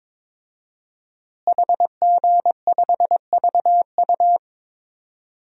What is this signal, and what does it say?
Morse code, heard as a keyed tone, about 22 wpm, 700 Hz. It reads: HG5VU
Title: DXCC-100 Call Signs - 5 Characters - Repeated 3x - 22wpm